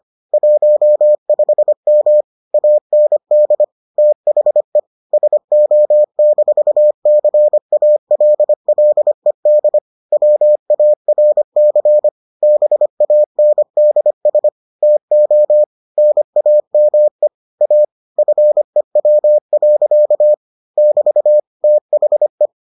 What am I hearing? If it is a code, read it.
Morse code, heard as a keyed tone, about 25 wpm, 600 Hz. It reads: 15M AND THE SO-CALLED WARC BANDS TO NAME A FEW. <BT> THE